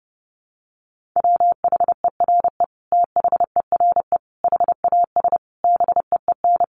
Morse code, heard as a keyed tone, about 30 wpm, 700 Hz. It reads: WHERE THERE HAS BEEN
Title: Sentences from Top 1400 Words 30wpm